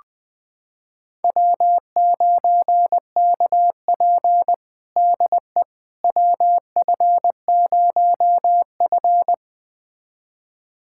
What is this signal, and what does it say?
Morse code, heard as a keyed tone, about 20 wpm, 700 Hz. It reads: W9KP DE WF0F
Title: USA Callsign DE Callsign - 4 Characters - 20wpm